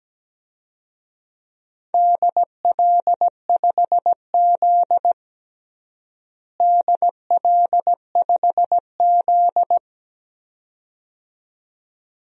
Morse code, heard as a keyed tone, about 17 wpm, 700 Hz. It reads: DL5Z DL5Z
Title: DXCC-100 Call Signs - 4 Characters - Repeated 2x - 17wpm